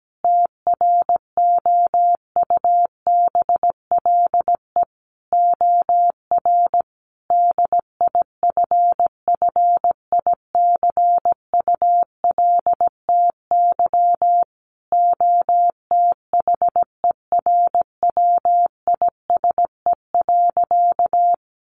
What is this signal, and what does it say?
Morse code, heard as a keyed tone, about 17 wpm, 700 Hz. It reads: TROUBLE OR DIFFICULTY OTHERWISE.
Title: Sentences from Top 1500 Words 17wpm